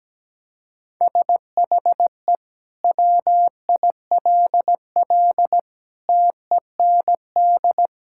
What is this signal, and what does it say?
Morse code, heard as a keyed tone, about 17 wpm, 700 Hz. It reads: SHE WILL TEND
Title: Sentences from Top 1000 Words - Review Part 5 - 17wpm